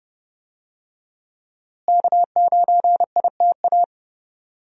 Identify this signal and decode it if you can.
Morse code, heard as a keyed tone, about 31 wpm, 700 Hz. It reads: K9ITA
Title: USA Call Signs - 5 Characters - Repeated 2x - 30wpm